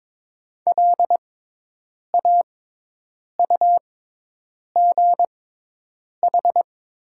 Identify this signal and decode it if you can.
Morse code, heard as a keyed tone, about 22 wpm, spaced as for 8 wpm, 700 Hz. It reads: LAUGH
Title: 6x Character Spacing - Top 500 Words - 22wpm